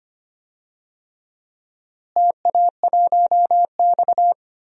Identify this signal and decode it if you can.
Morse code, heard as a keyed tone, about 25 wpm, 700 Hz. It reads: TA1X
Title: DXCC-100 Call Signs - 4 Characters - Repeated 3x - 25wpm